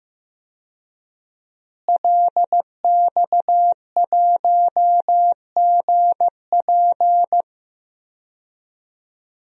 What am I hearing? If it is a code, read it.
Morse code, heard as a keyed tone, about 15 wpm, 700 Hz. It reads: LX1GP
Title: DXCC-100 Call Signs - 5 Characters - Repeated 2x - 15wpm